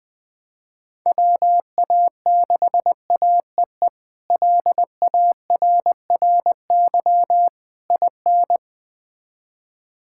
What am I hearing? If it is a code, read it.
Morse code, heard as a keyed tone, about 20 wpm, 700 Hz. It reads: WA6AEE LARRY IN
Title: K1USN SST QSO Practice - 20wpm